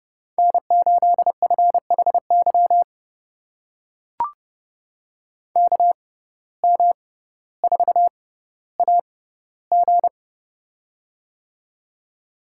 Morse code, heard as a keyed tone, about 32 wpm, spaced as for 16 wpm, 700 Hz. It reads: N8FHY KM4AG
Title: USA Call Signs - 5 Characters - 6x Character Spacing - 30wpm